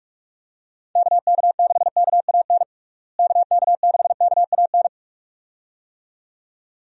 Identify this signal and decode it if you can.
Morse code, heard as a keyed tone, about 46 wpm, 700 Hz. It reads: KK6KAN KK6KAN
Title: USA Call Signs - 6 Characters - Repeated 2x - 45wpm